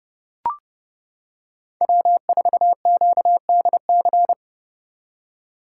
Morse code, heard as a keyed tone, about 29 wpm, 700 Hz. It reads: W4QDC